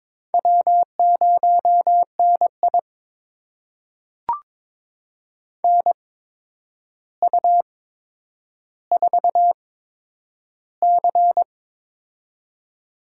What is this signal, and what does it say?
Morse code, heard as a keyed tone, about 22 wpm, spaced as for 9 wpm, 700 Hz. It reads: W0NI NU4C
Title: USA Call Signs - 4 Characters - 8x Character Spacing - 22wpm